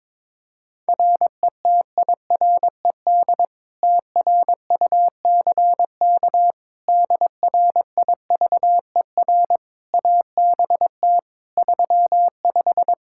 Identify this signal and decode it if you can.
Morse code, heard as a keyed tone, about 22 wpm, 700 Hz. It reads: RETIRED TRUCK DRIVER ABT 35
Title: Practice QSO - Exchange 3 - More Info - 22wpm